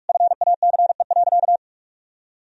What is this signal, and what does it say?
Morse code, heard as a keyed tone, about 46 wpm, 700 Hz. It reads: FACE.
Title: Sentences from Top 1000 Words - Review Part 2 - 45wpm